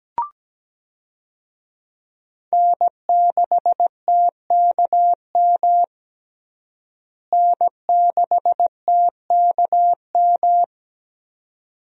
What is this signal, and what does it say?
Morse code, heard as a keyed tone, about 17 wpm, 700 Hz. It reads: N6TKM N6TKM
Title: USA Call Signs - 5 Characters - Repeated 2x - 17wpm